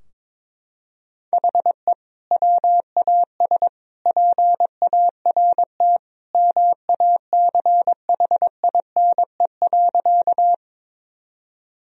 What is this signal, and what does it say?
Morse code, heard as a keyed tone, about 22 wpm, 700 Hz. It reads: HE WAS PART MACHINE.
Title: Sentences from Top 1000 Words - Review Part 2 - 22wpm